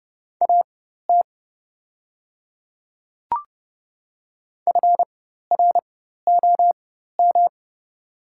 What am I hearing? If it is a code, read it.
Morse code, heard as a keyed tone, about 30 wpm, spaced as for 14 wpm, 700 Hz. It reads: AT FROM
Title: 4x Character Spacing - Top 200 Words - 30wpm